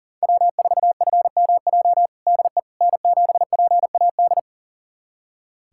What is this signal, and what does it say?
Morse code, heard as a keyed tone, about 40 wpm, 700 Hz. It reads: W4FMJ DE N7PAD
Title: USA Callsign DE Callsign - 5 Characters - 40wpm